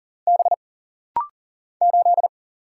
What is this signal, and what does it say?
Morse code, heard as a keyed tone, about 40 wpm, 700 Hz. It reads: B 8
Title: Single Letter-Number - Rapid-Fire 40wpm